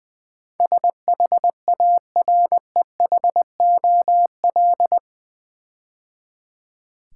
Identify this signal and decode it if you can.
Morse code, heard as a keyed tone, about 20 wpm, 700 Hz. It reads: SHAREHOL
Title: Build A Word - 4 Letters At A Time - 11-Letter Words - 20wpm